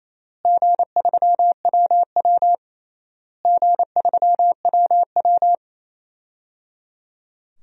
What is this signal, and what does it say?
Morse code, heard as a keyed tone, about 28 wpm, 700 Hz. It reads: G3WW G3WW
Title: DXCC-100 Call Signs - 4 Characters - Repeated 3x - 28wpm